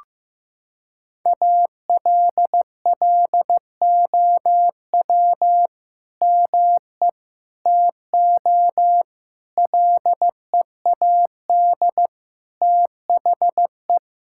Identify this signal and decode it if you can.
Morse code, heard as a keyed tone, about 15 wpm, 700 Hz. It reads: ALLOW ME TO LEAD THE
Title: Sentences from Top 1000 Words - Review Part 5 - 15wpm